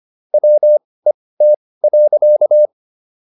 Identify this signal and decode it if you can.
Morse code, heard as a keyed tone, about 25 wpm, spaced as for 18 wpm, 600 Hz. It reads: WET.